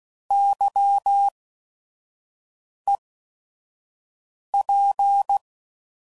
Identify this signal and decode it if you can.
Morse code, heard as a keyed tone, about 16 wpm, spaced as for 5 wpm, 800 Hz. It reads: YEP